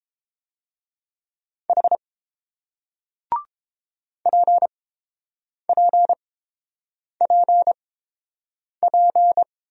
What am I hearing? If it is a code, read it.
Morse code, heard as a keyed tone, about 31 wpm, spaced as for 7 wpm, 700 Hz. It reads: H PPPP